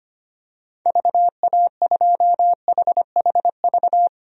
Morse code, heard as a keyed tone, about 25 wpm, 700 Hz. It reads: VA2HHV
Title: DXCC-100 Call Signs - 6 Characters - 25wpm